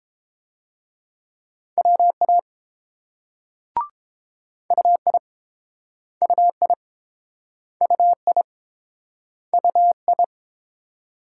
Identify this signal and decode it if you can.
Morse code, heard as a keyed tone, about 30 wpm, 700 Hz. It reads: WA UI UI UI UI